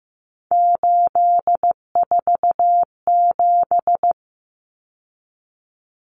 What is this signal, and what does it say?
Morse code, heard as a keyed tone, about 15 wpm, 700 Hz. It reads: 847
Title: Numbers - 3-Digit 15wpm